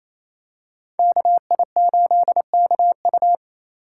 Morse code, heard as a keyed tone, about 28 wpm, 700 Hz. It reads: KI8KU
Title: USA Call Signs - 5 Characters - 8x Character Spacing - 28wpm